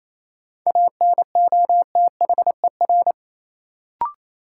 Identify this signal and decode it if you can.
Morse code, heard as a keyed tone, about 28 wpm, 700 Hz. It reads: ANOTHER
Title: Sentences from Top 600 Words - New Word Drill - 28wpm